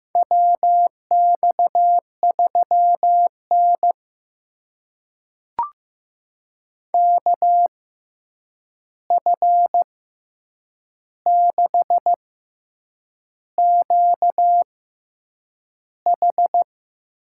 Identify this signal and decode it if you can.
Morse code, heard as a keyed tone, about 15 wpm, spaced as for 7 wpm, 700 Hz. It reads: WX3N KF6QH
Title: USA Call Signs - 6x Character Spacing - 15wpm